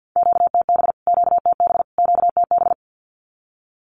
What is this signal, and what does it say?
Morse code, heard as a keyed tone, about 48 wpm, 700 Hz. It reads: QTB QTB QTB
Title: Q-Codes Repeated 3x 50wpm